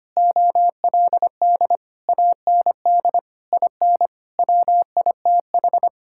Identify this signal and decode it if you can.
Morse code, heard as a keyed tone, about 25 wpm, 700 Hz. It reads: OLD AND IN WITH
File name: Sets of 9 Words - 25wpm